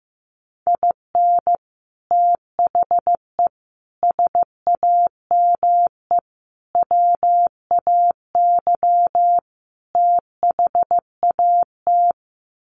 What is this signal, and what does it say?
Morse code, heard as a keyed tone, about 15 wpm, 700 Hz. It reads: IN THE SAME WAY THAT
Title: Sets of 5 Words 15wpm